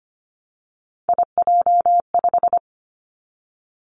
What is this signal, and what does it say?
Morse code, heard as a keyed tone, about 25 wpm, 700 Hz. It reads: IJ5
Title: Three Challenging Letter-Number Combinations 25wpm